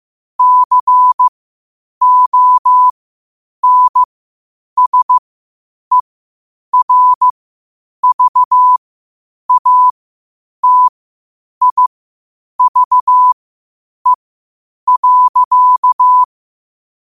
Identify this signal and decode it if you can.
Morse code, heard as a keyed tone, about 15 wpm, spaced as for 8 wpm, 1000 Hz. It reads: CONSERVATIVE.
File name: qotdfiles / 8wpm